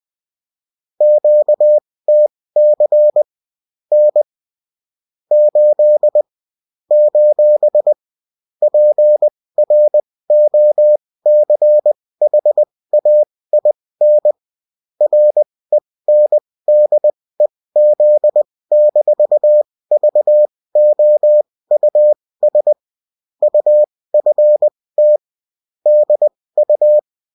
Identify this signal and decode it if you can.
Morse code, heard as a keyed tone, about 20 wpm, spaced as for 16 wpm, 600 Hz. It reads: QTC N 8 : PROCHAIN RENDEZ-VOUS UFT DU